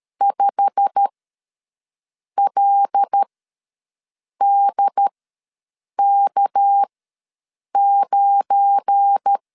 Morse code, heard as a keyed tone, about 13 wpm, spaced as for 6 wpm, 800 Hz. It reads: 5LDK9
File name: LEZIONI / 9 2 ws 12wpm 3406